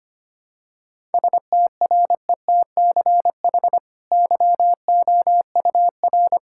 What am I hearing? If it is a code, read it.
Morse code, heard as a keyed tone, about 25 wpm, 690 Hz. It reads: STRETCH YOUR
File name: Easy Sentences with Hints - 25wpm